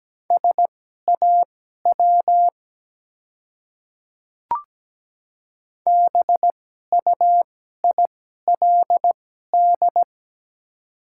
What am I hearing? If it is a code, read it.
Morse code, heard as a keyed tone, about 17 wpm, spaced as for 12 wpm, 700 Hz. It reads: SAW BUILD
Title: 2x Character Spacing - Top 300 Words - 17wpm